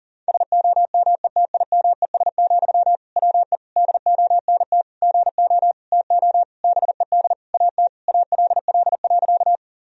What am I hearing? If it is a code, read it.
Morse code, heard as a keyed tone, about 40 wpm, 700 Hz. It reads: SOMETIMES, WE DONT GO TO BED AT ALL.
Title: Sentences from Top 1000 Words - Review Part 3 - 40wpm